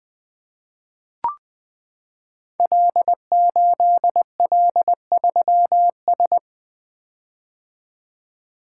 Morse code, heard as a keyed tone, about 20 wpm, 700 Hz. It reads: L8L3S
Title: Code Groups - 5 Letter:Numbers - 20wpm